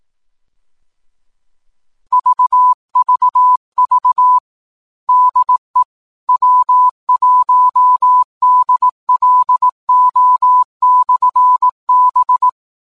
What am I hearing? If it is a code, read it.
Morse code, heard as a keyed tone, about 18 wpm, 1000 Hz. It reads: VVV DE W1DLO/B
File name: 10m-beacon